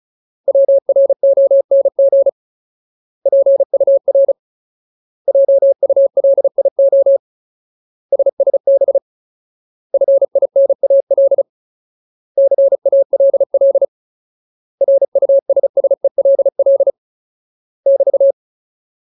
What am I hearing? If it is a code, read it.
Morse code, heard as a keyed tone, about 34 wpm, 550 Hz. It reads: WRONG PUR JULIO SSB FINAL CALL RUSSELL <BT>
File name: Ultimate Word List / 35WPM Ultimate